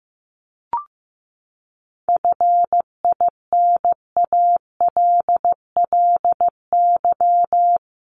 Morse code, heard as a keyed tone, about 15 wpm, 700 Hz. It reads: FINALLY
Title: Sentences from Top 1000 Words 15wpm